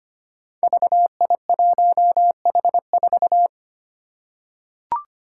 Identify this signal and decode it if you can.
Morse code, heard as a keyed tone, about 25 wpm, 700 Hz. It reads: VI1H4
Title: Five Challenging Letter-Number Combinations 25wpm